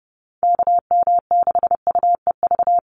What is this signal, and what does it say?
Morse code, heard as a keyed tone, about 30 wpm, 700 Hz. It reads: KM6UEV
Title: USA Call Signs 30wpm